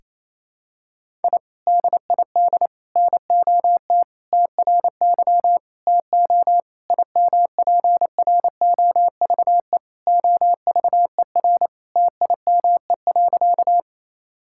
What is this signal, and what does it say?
Morse code, heard as a keyed tone, about 28 wpm, 700 Hz. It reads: I DID NOT TRY TO IMPROVE OVER TIME.
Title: Sentences from Top 1000 Words - Review Part 5 - 28wpm